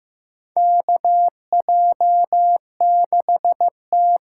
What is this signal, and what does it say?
Morse code, heard as a keyed tone, about 15 wpm, 700 Hz. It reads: KJ6T
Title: USA Call Signs - 6 Characters - 8x Character Spacing - 15wpm